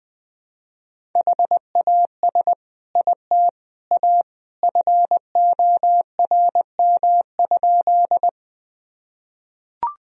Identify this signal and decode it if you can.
Morse code, heard as a keyed tone, about 20 wpm, 700 Hz. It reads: HAS IT A FORM?
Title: Sentences from Top 1000 Words - Review Part 1 - 20wpm